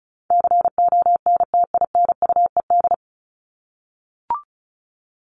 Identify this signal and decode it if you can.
Morse code, heard as a keyed tone, about 34 wpm, 700 Hz. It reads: CONTINUED
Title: ED Suffix - Top 100 Words with Base Words 35wpm